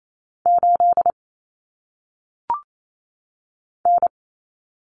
Morse code, heard as a keyed tone, about 28 wpm, 700 Hz. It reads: 8 N
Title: Common Cut Numbers - 1-Digit 28wpm